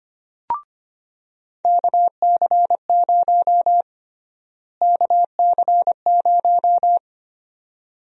Morse code, heard as a keyed tone, about 25 wpm, 700 Hz. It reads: KC0 KC0
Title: USA Call Sign Prefixes - Repeated 2x - 25wpm